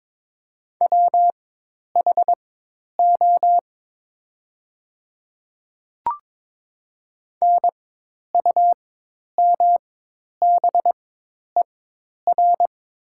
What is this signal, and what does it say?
Morse code, heard as a keyed tone, about 22 wpm, spaced as for 10 wpm, 700 Hz. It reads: WHO NUMBER
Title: 4x Character Spacing - Top 100 Words - 22wpm